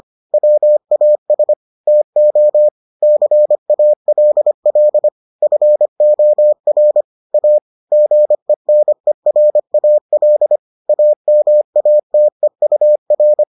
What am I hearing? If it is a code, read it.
Morse code, heard as a keyed tone, about 25 wpm, 600 Hz. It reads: WAS TO CALL FOR A GENERAL AMATEUR